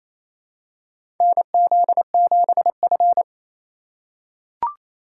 Morse code, heard as a keyed tone, about 28 wpm, 700 Hz. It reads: NZ7F
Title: USA Call Signs - 4 Characters - 4x Character Spacing - 28wpm